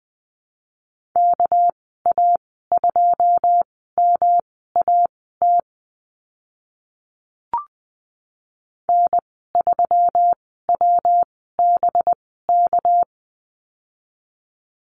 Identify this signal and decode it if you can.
Morse code, heard as a keyed tone, about 19 wpm, spaced as for 14 wpm, 700 Hz. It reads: KA2MAT N3WBK
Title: 200 Lesson 32g - Letter k - Call Signs 20wpm